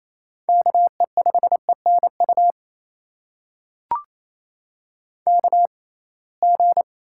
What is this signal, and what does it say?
Morse code, heard as a keyed tone, about 28 wpm, spaced as for 21 wpm, 700 Hz. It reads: KE5ENU KG
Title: USA Call Signs - 6 Characters - 6x Character Spacing - 28wpm